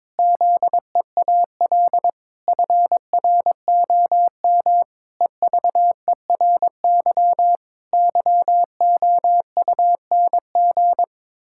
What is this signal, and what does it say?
Morse code, heard as a keyed tone, about 22 wpm, 700 Hz. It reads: ZEAL FROM EVERY YOUNG
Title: Pangram Challenge v2 - 22wpm